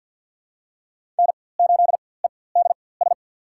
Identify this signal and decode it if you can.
Morse code, heard as a keyed tone, about 51 wpm, spaced as for 23 wpm, 700 Hz. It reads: N8EDS
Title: USA Call Signs - 4x Character Spacing - 50wpm